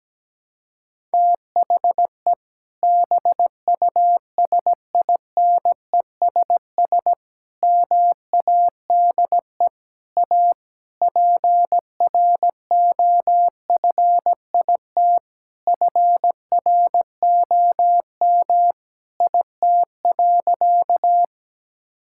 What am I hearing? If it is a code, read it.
Morse code, heard as a keyed tone, about 17 wpm, 700 Hz. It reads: THE BUSINESS MADE A PROFIT FROM IT.
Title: Sentences from Top 1000 Words - Review Part 5 - 17wpm